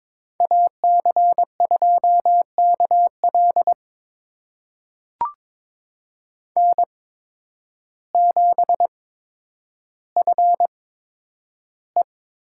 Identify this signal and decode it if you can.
Morse code, heard as a keyed tone, about 22 wpm, spaced as for 10 wpm, 700 Hz. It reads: AC2KL N7FE
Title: USA Call Signs - 5 Characters - 8x Character Spacing - 22wpm